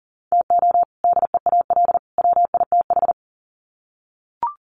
Morse code, heard as a keyed tone, about 40 wpm, 700 Hz. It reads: TO DEAL WITH
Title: Sets of 3 Words 40wpm